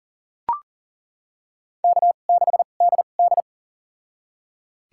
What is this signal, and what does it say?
Morse code, heard as a keyed tone, about 40 wpm, spaced as for 29 wpm, 700 Hz. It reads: K6DD
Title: USA Call Signs - 4 Characters - 2x Character Spacing - 40wpm